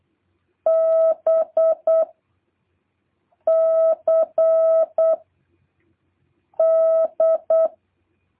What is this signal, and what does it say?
Morse code, heard as a keyed tone, about 8 wpm, 660 Hz. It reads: B C D